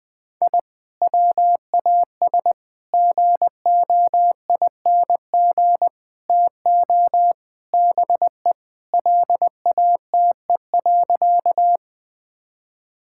Morse code, heard as a keyed tone, about 21 wpm, 700 Hz. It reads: I WAS GOING TO BE LATE.
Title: Sentences with 5-Word Phrases - 20wpm